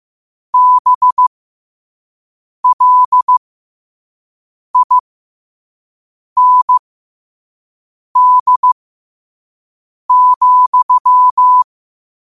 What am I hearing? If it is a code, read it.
Morse code, heard as a keyed tone, about 15 wpm, spaced as for 5 wpm, 1000 Hz. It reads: BLIND,
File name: qotdfiles / 5wpm